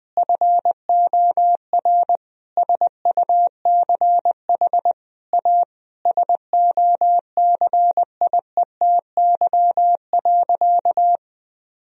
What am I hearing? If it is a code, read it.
Morse code, heard as a keyed tone, about 20 wpm, 700 Hz. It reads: FOR SUCH A SOCIETY.
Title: Sentences from Top 1000 Words - Review Part 4 - 20wpm